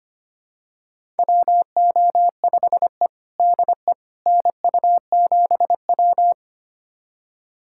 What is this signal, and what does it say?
Morse code, heard as a keyed tone, about 25 wpm, 700 Hz. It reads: WO5E DE NU7W
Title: USA Callsign DE Callsign - 4 Characters - 25wpm